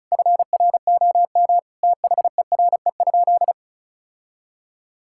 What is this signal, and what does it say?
Morse code, heard as a keyed tone, about 34 wpm, 700 Hz. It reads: FROM THERE?
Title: Sentences from Top 1000 Words - Review Part 5 - 35wpm